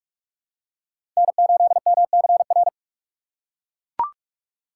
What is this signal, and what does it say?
Morse code, heard as a keyed tone, about 41 wpm, 700 Hz. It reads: N8MCR
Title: USA Call Signs - 4x Character Spacing - 45wpm